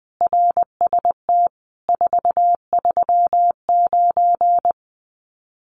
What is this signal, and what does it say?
Morse code, heard as a keyed tone, about 20 wpm, 700 Hz. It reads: RST 439
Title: Common QSO Phrases 20wpm